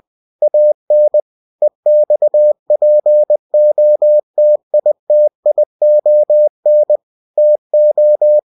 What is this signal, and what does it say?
Morse code, heard as a keyed tone, about 20 wpm, 600 Hz. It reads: AN EXPOTITION TO
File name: cw-books / winnie-the-pooh / winnie-20wpm-page-0093